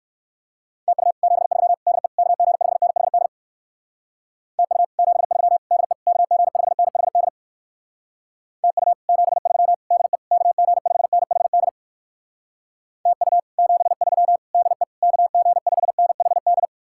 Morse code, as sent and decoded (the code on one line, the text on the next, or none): - ..- / --... ...-- / -.. / -.- --. ..... -. .... -.. / - ..- / --... ...-- / -.. / -.- --. ..... -. .... -.. / - ..- / --... ...-- / -.. / -.- --. ..... -. .... -.. / - ..- / --... ...-- / -.. / -.- --. ..... -. .... -..
TU 73 DE KG5NHD TU 73 DE KG5NHD TU 73 DE KG5NHD TU 73 DE KG5NHD